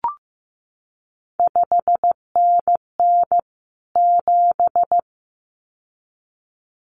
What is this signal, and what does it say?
Morse code, heard as a keyed tone, about 15 wpm, 700 Hz. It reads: 5NN 7
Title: CQ WW DX Contest Exchange Encore - 15wpm